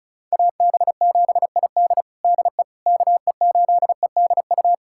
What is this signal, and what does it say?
Morse code, heard as a keyed tone, about 34 wpm, 700 Hz. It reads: AB7ID DE KE8EDU
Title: USA Callsign DE Callsign 35wpm